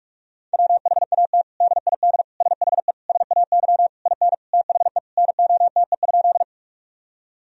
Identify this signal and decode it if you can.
Morse code, heard as a keyed tone, about 46 wpm, 700 Hz. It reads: WHAT DID SHE SAY IN THE NOTE?
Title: Sentences from Top 1000 Words - Review Part 2 - 45wpm